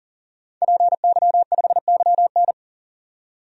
Morse code, heard as a keyed tone, about 40 wpm, 700 Hz. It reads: PY5YN
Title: DXCC-100 Call Signs - 5 Characters - 40wpm